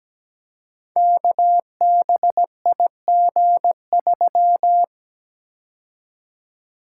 Morse code, heard as a keyed tone, about 17 wpm, 700 Hz. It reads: KBIG3
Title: Code Groups - 5 Letter:Numbers - 17wpm